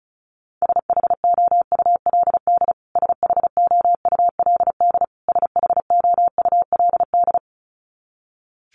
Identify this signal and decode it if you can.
Morse code, heard as a keyed tone, about 34 wpm, 700 Hz. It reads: SHOULD SHOULD SHOULD
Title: Top 500 Words Repeated 35wpm